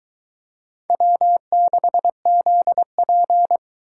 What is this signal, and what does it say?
Morse code, heard as a keyed tone, about 23 wpm, 700 Hz. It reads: W6ZP